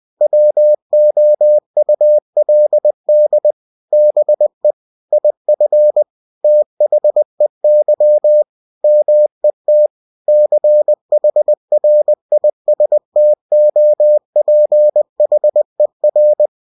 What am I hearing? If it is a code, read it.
Morse code, heard as a keyed tone, about 20 wpm, 600 Hz. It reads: WOULD BE IF THEY MET CHRISTOPHER
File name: cw-books / winnie-the-pooh / winnie-20wpm-page-0023